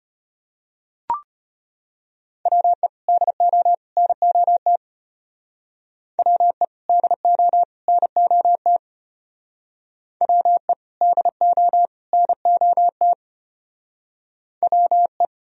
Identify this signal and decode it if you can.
Morse code, heard as a keyed tone, about 32 wpm, 700 Hz. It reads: WE DO NOT WE DO NOT WE DO NOT WE